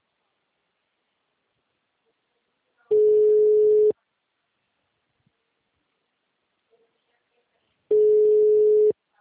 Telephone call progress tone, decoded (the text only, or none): ringback tone